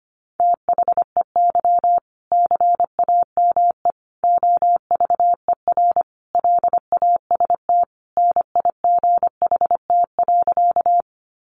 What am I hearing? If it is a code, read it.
Morse code, heard as a keyed tone, about 25 wpm, 700 Hz. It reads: THEY CAME OVER LAST NIGHT.
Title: Sentences from Top 300 Words 25wpm